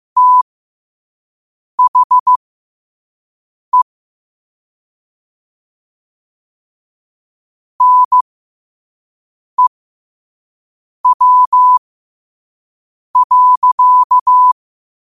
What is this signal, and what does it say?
Morse code, heard as a keyed tone, about 15 wpm, spaced as for 5 wpm, 1000 Hz. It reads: THE NEW.
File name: qotdfiles / 5wpm